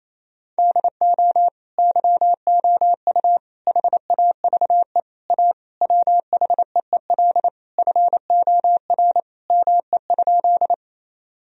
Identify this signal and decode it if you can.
Morse code, heard as a keyed tone, about 28 wpm, 700 Hz. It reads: DO YOU HAVE A WHEEL FOR ME?
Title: Sentences from Top 1000 Words - Review Part 3 - 28wpm